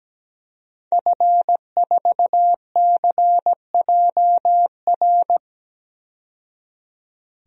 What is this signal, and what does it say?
Morse code, heard as a keyed tone, about 17 wpm, 700 Hz. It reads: F4CJR
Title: DXCC-100 Call Signs - 5 Characters - 17wpm